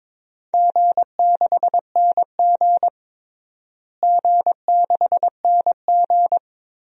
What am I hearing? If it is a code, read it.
Morse code, heard as a keyed tone, about 22 wpm, 700 Hz. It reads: G6NG G6NG
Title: DXCC-100 Call Signs - 4 Characters - Repeated 2x - 22wpm